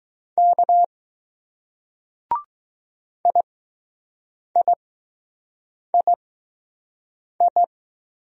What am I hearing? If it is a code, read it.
Morse code, heard as a keyed tone, about 20 wpm, spaced as for 6 wpm, 700 Hz. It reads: K IIII